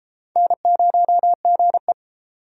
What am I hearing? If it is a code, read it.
Morse code, heard as a keyed tone, about 35 wpm, 700 Hz. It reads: N0GE
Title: Speed Racing - USA 1x2 Call Signs 28wpm